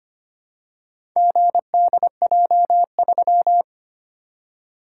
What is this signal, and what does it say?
Morse code, heard as a keyed tone, about 25 wpm, 700 Hz. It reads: GDJ3
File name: practice / 25wpm / Four Letter-Number Combinations - 25wpm